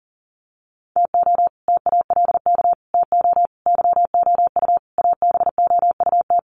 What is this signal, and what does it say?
Morse code, heard as a keyed tone, about 40 wpm, 700 Hz. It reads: TO TALK TO YOU ABOUT
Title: Sets of 5 Words Encore 40wpm